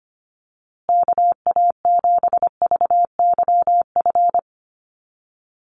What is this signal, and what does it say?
Morse code, heard as a keyed tone, about 25 wpm, 700 Hz. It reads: KA7VYF
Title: USA Call Signs Repeated 3x 25wpm